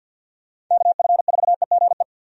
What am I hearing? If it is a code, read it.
Morse code, heard as a keyed tone, about 47 wpm, 700 Hz. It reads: KF4EGE
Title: USA Call Signs - 6 Characters - 50wpm